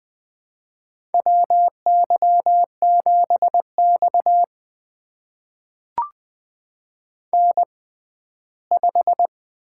Morse code, heard as a keyed tone, about 20 wpm, spaced as for 14 wpm, 700 Hz. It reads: WY7X N5
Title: USA Call Signs - 4 Characters - 6x Character Spacing - 20wpm